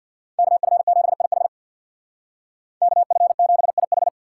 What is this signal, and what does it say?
Morse code, heard as a keyed tone, about 55 wpm, 700 Hz. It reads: KF7IH KF7IH
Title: Speed Racing - USA 2x2 Call Signs 50wpm